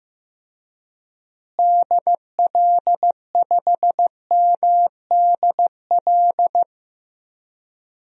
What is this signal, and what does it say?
Morse code, heard as a keyed tone, about 15 wpm, 700 Hz. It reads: DL5MDL
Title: DXCC-100 Call Signs - 6 Characters - Repeated 2x - 15wpm